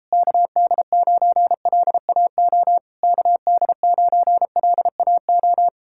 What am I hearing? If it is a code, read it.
Morse code, heard as a keyed tone, about 38 wpm, 700 Hz. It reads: KD9LAO KD9LAO